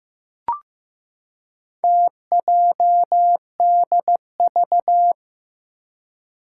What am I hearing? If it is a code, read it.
Morse code, heard as a keyed tone, about 15 wpm, 700 Hz. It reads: TJDV
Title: Four Challenging Letter-Number Combinations 15wpm